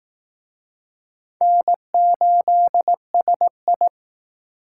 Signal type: Morse code

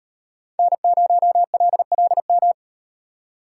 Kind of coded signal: Morse code

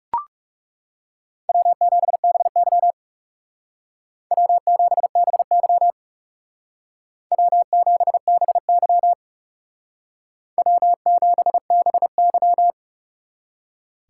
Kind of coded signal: Morse code

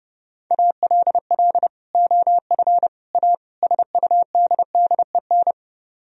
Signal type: Morse code

